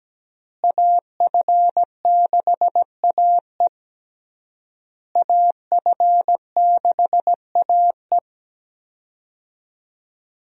Morse code, heard as a keyed tone, about 17 wpm, 700 Hz. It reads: AF6AE AF6AE
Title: USA Call Signs - 5 Characters - Repeated 3x - 17wpm